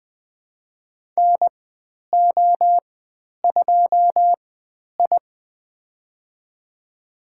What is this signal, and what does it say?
Morse code, heard as a keyed tone, about 20 wpm, spaced as for 10 wpm, 700 Hz. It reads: NO2I
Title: 109 Lesson 18g - Letter m - Call Signs 20wpm